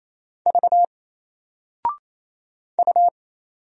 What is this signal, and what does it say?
Morse code, heard as a keyed tone, about 27 wpm, 700 Hz. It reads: V U
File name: Single-Letter-Number-28wpm